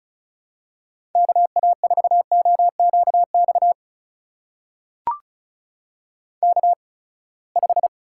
Morse code, heard as a keyed tone, about 36 wpm, spaced as for 24 wpm, 700 Hz. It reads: KA4OQX K5
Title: USA Call Signs - 8x Character Spacing - 35wpm